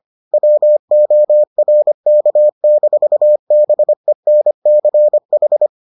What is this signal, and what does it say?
Morse code, heard as a keyed tone, about 25 wpm, 600 Hz. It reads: WORK-BENCH